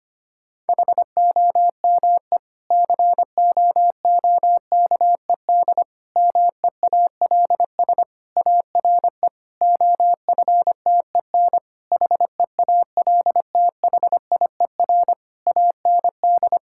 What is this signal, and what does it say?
Morse code, heard as a keyed tone, about 25 wpm, 700 Hz. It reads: HOME COOKED MEALS ARE OFTEN HEALTHIER AND
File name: practice / 25wpm / Difficult Sentences with Hints - 25wpm